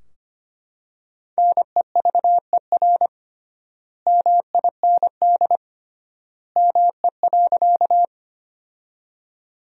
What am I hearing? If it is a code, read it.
Morse code, heard as a keyed tone, about 25 wpm, 700 Hz. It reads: NEVER MIND ME.
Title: 3x Word-Spacing - Sentences from Top 500 Words - 25wpm